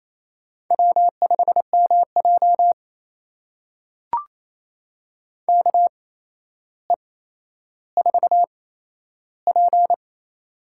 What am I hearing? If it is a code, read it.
Morse code, heard as a keyed tone, about 27 wpm, spaced as for 12 wpm, 700 Hz. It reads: W5MJ KE4P
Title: USA Call Signs - 4 Characters - 8x Character Spacing - 28wpm